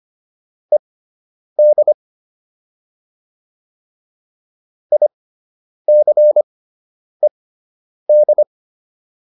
Morse code, heard as a keyed tone, about 25 wpm, spaced as for 9 wpm, 600 Hz. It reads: ED ICED